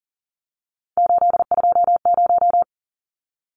Morse code, heard as a keyed tone, about 40 wpm, 700 Hz. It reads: 820
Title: Numbers - 3-Digit 40wpm